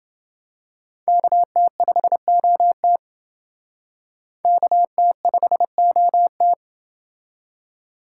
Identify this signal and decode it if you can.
Morse code, heard as a keyed tone, about 28 wpm, 700 Hz. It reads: KT5OT KT5OT